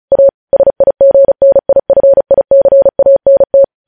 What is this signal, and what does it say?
Morse code, heard as a keyed tone, about 36 wpm, 550 Hz. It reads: A SIGNIFICANT
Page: news headlines in morse code at 35 wpm